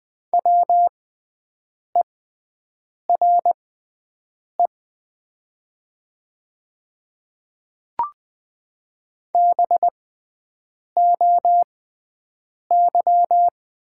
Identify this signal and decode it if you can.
Morse code, heard as a keyed tone, about 20 wpm, spaced as for 7 wpm, 700 Hz. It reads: WERE BOY
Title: 6x Character Spacing - Top 200 Words - 20wpm